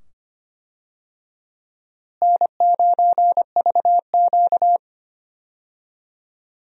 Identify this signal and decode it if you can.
Morse code, heard as a keyed tone, about 25 wpm, 700 Hz. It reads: N9VQ
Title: USA Call Signs - 4 Characters - Repeated 2x - 25wpm